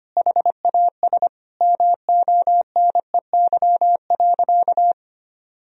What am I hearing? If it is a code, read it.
Morse code, heard as a keyed tone, about 25 wpm, 700 Hz. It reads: HAS MONEY.
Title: Sentences from Top 1000 Words - Review Part 3 - 25wpm